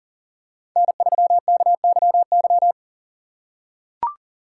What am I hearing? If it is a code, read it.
Morse code, heard as a keyed tone, about 40 wpm, 700 Hz. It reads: N3KYY
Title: USA Call Signs - 5 Characters - 4x Character Spacing - 40wpm